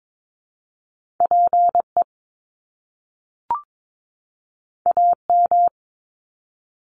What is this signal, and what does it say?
Morse code, heard as a keyed tone, about 22 wpm, 700 Hz. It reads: PE AM
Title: Top Two Letter Combinations in Words 22wpm